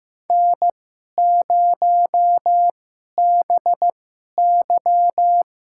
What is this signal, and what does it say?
Morse code, heard as a keyed tone, about 15 wpm, spaced as for 11 wpm, 700 Hz. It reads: N0BY
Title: USA Call Signs - 4 Characters - 2x Character Spacing - 15wpm